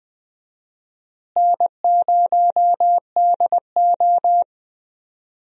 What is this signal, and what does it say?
Morse code, heard as a keyed tone, about 20 wpm, 700 Hz. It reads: N0DO
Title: USA Call Signs - 4 Characters - Repeated 2x - 20wpm